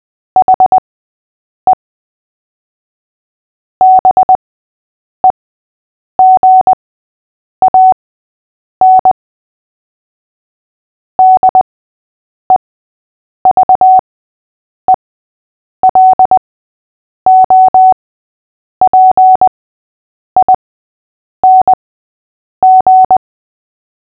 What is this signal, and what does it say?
Morse code, heard as a keyed tone, about 20 wpm, spaced as for 8 wpm, 750 Hz. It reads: HE BEGAN DEVELOPING
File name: cw / fists / Short-Story-6-8wpm